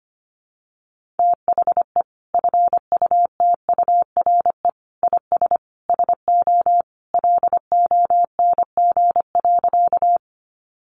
Morse code, heard as a keyed tone, about 25 wpm, 700 Hz. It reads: THE FUTURE IS SO LONG.
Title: Sentences from Top 700 Words 25wpm